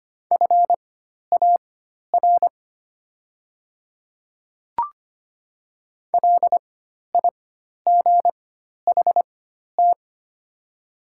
Morse code, heard as a keyed tone, about 25 wpm, spaced as for 12 wpm, 700 Hz. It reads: FAR LIGHT
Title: 4x Character Spacing - Top 300 Words - 25wpm